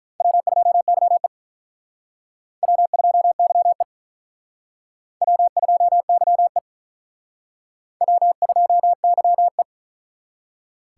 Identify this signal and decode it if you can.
Morse code, heard as a keyed tone, about 44 wpm, 700 Hz. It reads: W2YE W2YE W2YE W2YE